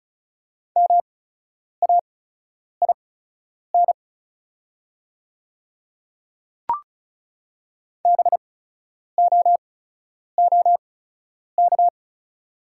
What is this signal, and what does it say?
Morse code, heard as a keyed tone, about 34 wpm, spaced as for 10 wpm, 690 Hz. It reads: MAIN BOOK
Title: 8x Character Spacing - Top 300 Words - 35wpm